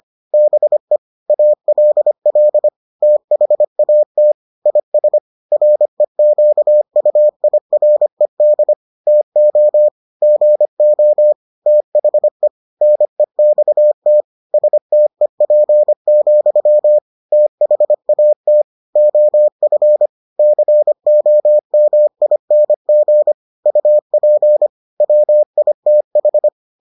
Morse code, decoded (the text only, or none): BE ALL THAT IS REQUIRED TO GO THE NEXT STEP, THAT OF COMING UP WITH